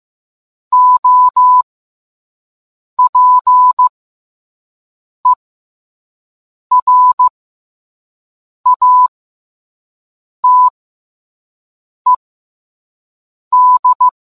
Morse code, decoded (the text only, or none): OPERATED